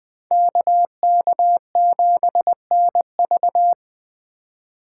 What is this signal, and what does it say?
Morse code, heard as a keyed tone, about 23 wpm, 700 Hz. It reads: KK7NV
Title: Speed Racing - USA Call Signs 20wpm